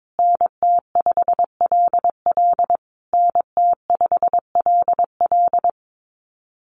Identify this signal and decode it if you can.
Morse code, heard as a keyed tone, about 22 wpm, 700 Hz. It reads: NT5LL NT5LL
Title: USA Call Signs Repeated 3x 22wpm